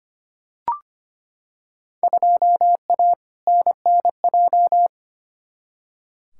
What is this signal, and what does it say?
Morse code, heard as a keyed tone, about 25 wpm, 700 Hz. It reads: 2A NNJ
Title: Field Day Exchanges - 25wpm